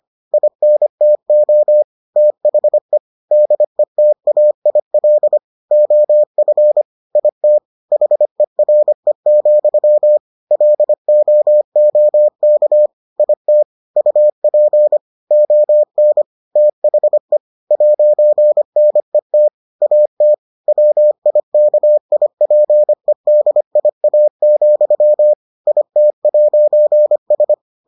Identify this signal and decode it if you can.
Morse code, heard as a keyed tone, about 25 wpm, 600 Hz. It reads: INTO THE DETAIL OF IT HERE, LOOK IT UP ON THE 'NET AT WIKIPEDIA, IT'S